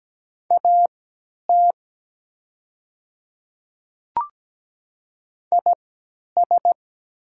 Morse code, heard as a keyed tone, about 17 wpm, spaced as for 10 wpm, 700 Hz. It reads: AT IS
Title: 3x Character Spacing - Top 100 Words - 17wpm